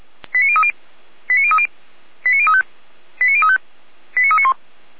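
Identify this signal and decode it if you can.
Five-tone selective call (ZVEI), heard as quick runs of five tones, tones of about 70 ms. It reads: 89030 89030 89035 89035 89391
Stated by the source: Five-tone ZVEI selective call 89376, received on SDR.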